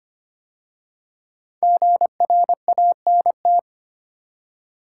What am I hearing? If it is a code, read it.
Morse code, heard as a keyed tone, about 25 wpm, 700 Hz. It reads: GRANT
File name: 5 Letter Words - First 3 Letters Given - 25wpm